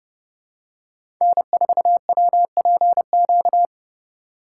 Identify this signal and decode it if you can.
Morse code, heard as a keyed tone, about 32 wpm, 700 Hz. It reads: N4WPQ
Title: USA Call Signs - 5 Characters - 2x Character Spacing - 30wpm